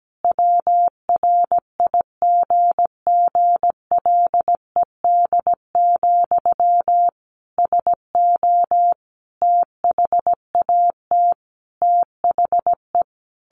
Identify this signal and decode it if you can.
Morse code, heard as a keyed tone, about 17 wpm, 700 Hz. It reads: WRIGGLED, SO THAT THE
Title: A Sound Of Thunder 17wpm